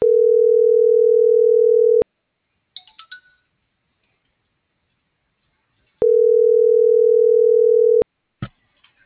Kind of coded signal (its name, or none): telephone call progress tone